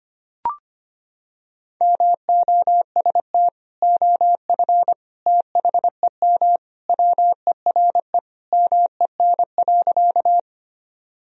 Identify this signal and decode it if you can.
Morse code, heard as a keyed tone, about 25 wpm, 700 Hz. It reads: MOST OF THEM WERE MEN.
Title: Sentences from Top 1000 Words - Review Part 1 - 25wpm